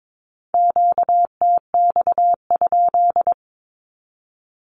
Morse code, heard as a keyed tone, about 22 wpm, 700 Hz. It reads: QTX?
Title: Q-Codes with Spoken Meaning - 22wpm